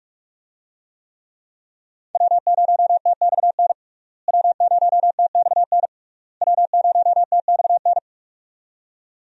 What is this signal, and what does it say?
Morse code, heard as a keyed tone, about 46 wpm, 700 Hz. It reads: W0TXN W0TXN W0TXN
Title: USA Call Signs - 5 Characters - Repeated 3x - 45wpm